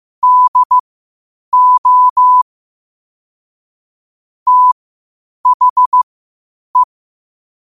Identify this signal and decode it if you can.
Morse code, heard as a keyed tone, about 15 wpm, spaced as for 8 wpm, 1000 Hz. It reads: DO THE
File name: qotdfiles / 8wpm